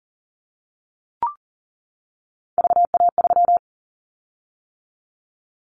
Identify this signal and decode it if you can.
Morse code, heard as a keyed tone, about 40 wpm, 700 Hz. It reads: VA3
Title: Callsign Prefixes for Top 100 DXCC - 40wpm